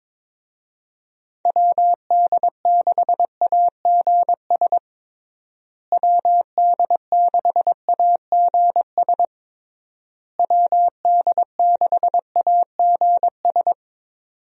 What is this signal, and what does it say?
Morse code, heard as a keyed tone, about 22 wpm, 700 Hz. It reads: WD6AGS WD6AGS WD6AGS
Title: USA Call Signs - 6 Characters - Repeated 3x - 22wpm